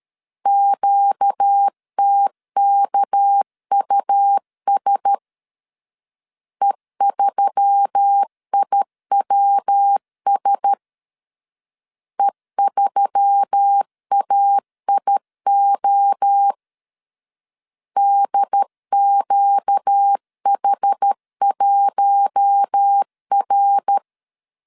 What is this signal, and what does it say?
Morse code, heard as a keyed tone, about 13 wpm, 800 Hz. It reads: QTKUS E3IWS E3AIO DQH1R
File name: LEZIONI / 6 PKX0 / 6 PK 12wpm 2006